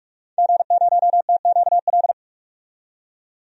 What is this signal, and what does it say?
Morse code, heard as a keyed tone, about 42 wpm, 700 Hz. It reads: G0TQL
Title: DXCC-100 Call Signs - 5 Characters - 45wpm